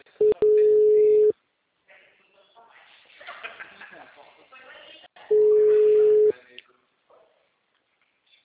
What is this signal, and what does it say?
Ringback tone (Europe, 425 Hz), with speech.
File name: out- -2013- - - .7600